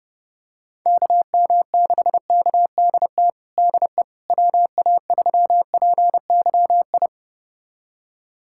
Morse code, heard as a keyed tone, about 32 wpm, 700 Hz. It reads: KM6KDT DE WA3PYI
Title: USA Callsign DE Callsign - 6 Characters - 30wpm